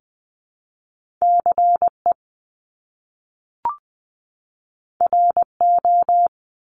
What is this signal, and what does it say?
Morse code, heard as a keyed tone, about 20 wpm, 700 Hz. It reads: CE RO
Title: Top Two Letter Combinations in Words 20wpm